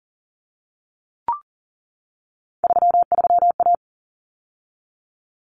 Morse code, heard as a keyed tone, about 40 wpm, 700 Hz. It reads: 33A
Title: Common Cut Numbers - 3-Digit 40wpm